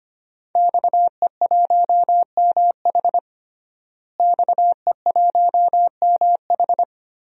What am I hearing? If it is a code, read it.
Morse code, heard as a keyed tone, about 25 wpm, 700 Hz. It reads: XE1MH XE1MH
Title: DXCC-100 Call Signs - 5 Characters - Repeated 3x - 25wpm